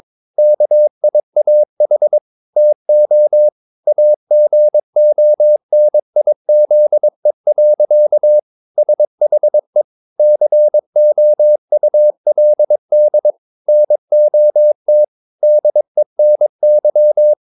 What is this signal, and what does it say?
Morse code, heard as a keyed tone, about 22 wpm, 600 Hz. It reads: KIAH TO AGONIZE. SHE COULD NOT DENY
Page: CW med Gnister